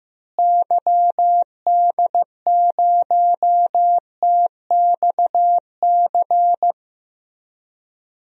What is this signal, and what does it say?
Morse code, heard as a keyed tone, about 15 wpm, 700 Hz. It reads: YD0TXC
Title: DXCC-100 Call Signs - 6 Characters - Repeated 3x - 15wpm